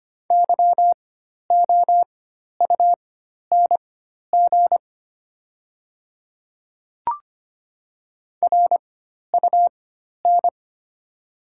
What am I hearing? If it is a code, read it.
Morse code, heard as a keyed tone, about 25 wpm, spaced as for 12 wpm, 690 Hz. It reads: YOUNG RUN
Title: 4x Character Spacing - Top 500 Words - 25wpm